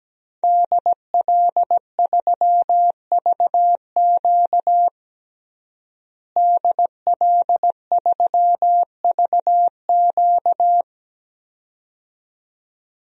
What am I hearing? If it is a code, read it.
Morse code, heard as a keyed tone, about 17 wpm, 700 Hz. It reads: DL3VQ DL3VQ
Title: DXCC-100 Call Signs - 5 Characters - Repeated 2x - 17wpm